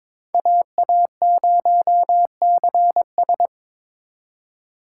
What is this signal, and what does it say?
Morse code, heard as a keyed tone, about 26 wpm, 700 Hz. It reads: AA0CS
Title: Speed Racing - USA 2x2 Call Signs 22wpm